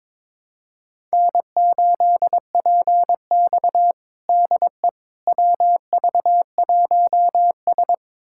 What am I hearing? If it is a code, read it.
Morse code, heard as a keyed tone, about 22 wpm, 700 Hz. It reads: N8PX DE WV1S
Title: USA Callsign DE Callsign - 4 Characters - 22wpm